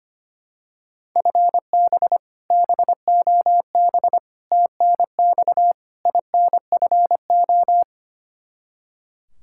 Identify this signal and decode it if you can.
Morse code, heard as a keyed tone, about 25 wpm, 700 Hz. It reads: FB BOB TNX INFO
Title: K1USN SST QSO Practice 25wpm